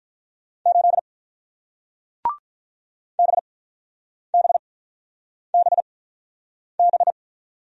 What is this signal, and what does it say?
Morse code, heard as a keyed tone, about 46 wpm, spaced as for 9 wpm, 700 Hz. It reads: 8 BBBB